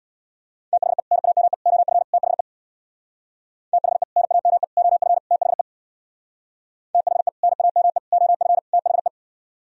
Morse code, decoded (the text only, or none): THE NAME OF THE THE NAME OF THE THE NAME OF THE